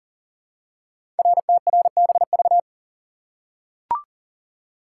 Morse code, heard as a keyed tone, about 41 wpm, 700 Hz. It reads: RTRBV
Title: Code Groups - 5 Letters - 40wpm